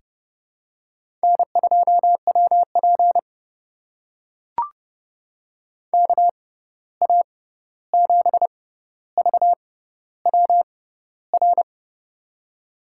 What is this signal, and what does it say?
Morse code, heard as a keyed tone, about 30 wpm, spaced as for 14 wpm, 700 Hz. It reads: N2WP KA7VWR
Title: USA Call Signs - 6x Character Spacing - 30wpm